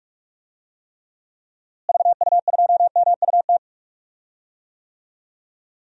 Morse code, heard as a keyed tone, about 42 wpm, 700 Hz. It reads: VU2MUT
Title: DXCC-100 Call Signs - 6 Characters - Repeated 2x - 45wpm